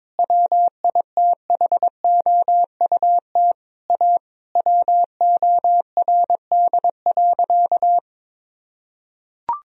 Morse code, heard as a keyed tone, about 21 wpm, 700 Hz. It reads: WITHOUT A WORD.
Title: Sentences from Top 1000 Words - Review Part 3 - 22wpm